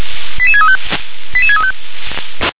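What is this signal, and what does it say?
Five-tone selective call (ZVEI), heard as quick runs of five tones, tones of about 70 ms. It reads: 88635 88635
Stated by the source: ZVEI five-tone selective call received on an SDR.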